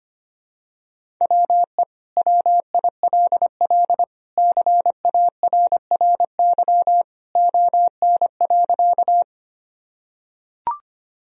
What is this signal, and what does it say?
Morse code, heard as a keyed tone, about 25 wpm, 700 Hz. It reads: WE WILL CARRY ON.
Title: Sentences from Top 1000 Words - Review Part 1 - 25wpm